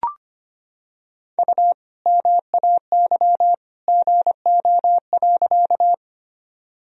Morse code, heard as a keyed tone, about 25 wpm, 700 Hz. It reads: U MAY GO.
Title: Sentences from Top 100 Words and Abbreviations 25wpm